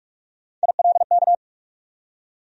Morse code, heard as a keyed tone, about 42 wpm, 700 Hz. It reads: ILK
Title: USA Call Sign Postfixes - 3 Letters 45wpm